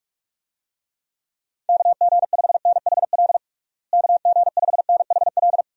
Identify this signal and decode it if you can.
Morse code, heard as a keyed tone, about 46 wpm, 700 Hz. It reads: KG5NHL KG5NHL
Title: USA Call Signs - 6 Characters - Repeated 2x - 45wpm